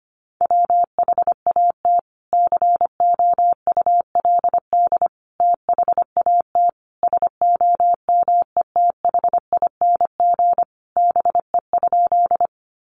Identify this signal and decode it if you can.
Morse code, heard as a keyed tone, about 25 wpm, 700 Hz. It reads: WHAT COULD THAT SOMETHING BE?
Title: Sentences from Top 600 Words 25wpm